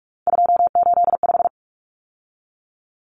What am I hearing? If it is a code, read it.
Morse code, heard as a keyed tone, about 44 wpm, 700 Hz. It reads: 285
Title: Common Cut Numbers - 3-Digit 45wpm